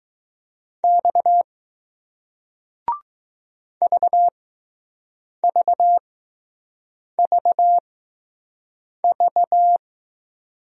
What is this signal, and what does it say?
Morse code, heard as a keyed tone, about 20 wpm, spaced as for 6 wpm, 700 Hz. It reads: X VVVV